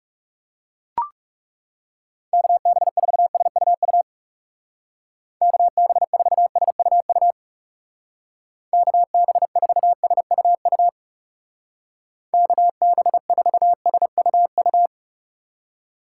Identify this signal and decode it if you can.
Morse code, heard as a keyed tone, about 36 wpm, 700 Hz. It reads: KB4SUU KB4SUU KB4SUU KB4SUU
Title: Speed Racing - USA 2x3 Call Signs 30wpm